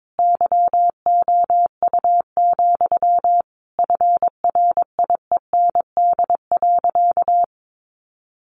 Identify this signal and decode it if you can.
Morse code, heard as a keyed tone, about 22 wpm, 700 Hz. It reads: YOU, FRIEND.
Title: Sentences from Top 1500 Words 22wpm